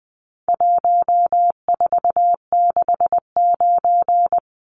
Morse code, heard as a keyed tone, about 20 wpm, 700 Hz. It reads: 1469
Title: Numbers - 4-Digit 20wpm